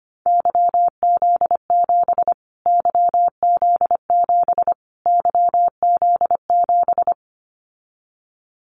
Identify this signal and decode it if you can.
Morse code, heard as a keyed tone, about 25 wpm, 700 Hz. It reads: YZ7 YZ7 YZ7
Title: Callsign Prefixes for Top 25 DXCC - Repeated 3x - 25wpm